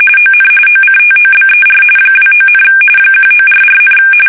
RTTY, about 45.45 baud. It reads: ON OCTOBER 29, 1944,
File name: fsk850 / battleshipnj